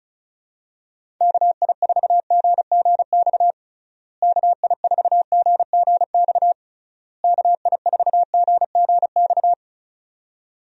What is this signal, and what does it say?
Morse code, heard as a keyed tone, about 34 wpm, 700 Hz. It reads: KI4GGX KI4GGX KI4GGX
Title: USA Call Signs - 6 Characters - Repeated 3x - 35wpm